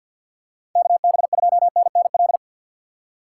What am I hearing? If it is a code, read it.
Morse code, heard as a keyed tone, about 52 wpm, 690 Hz. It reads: KB2NNL
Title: USA Call Signs - 2x Character Spacing - 50wpm